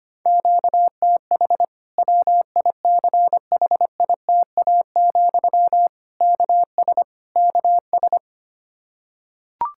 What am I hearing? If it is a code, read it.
Morse code, heard as a keyed tone, about 25 wpm, 700 Hz. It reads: QTH WICHITA, KS KS
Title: K1USN SST QSO Practice 25wpm